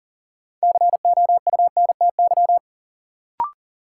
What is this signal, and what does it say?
Morse code, heard as a keyed tone, about 37 wpm, 700 Hz. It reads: COUNTY
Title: Sentences from Top 1200 Words - New Word Drill Alt - 40wpm